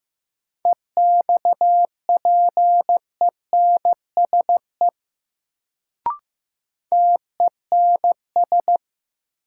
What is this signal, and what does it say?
Morse code, heard as a keyed tone, about 15 wpm, 700 Hz. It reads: EXPENSE TENS
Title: Sets of 4 Words - New Word Drill - 15wpm